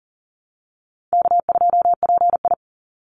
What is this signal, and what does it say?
Morse code, heard as a keyed tone, about 42 wpm, 700 Hz. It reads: K2PI
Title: Call Signs - 40wpm